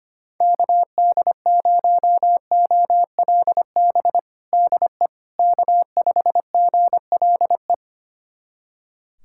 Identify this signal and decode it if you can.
Morse code, heard as a keyed tone, about 25 wpm, 700 Hz. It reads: KD0OLB DE K5GLE
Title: USA Callsign DE Callsign 25wpm